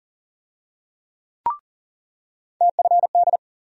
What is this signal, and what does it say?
Morse code, heard as a keyed tone, about 40 wpm, 700 Hz. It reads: TFD
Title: USA Call Sign Postfixes - 3 Letters 40wpm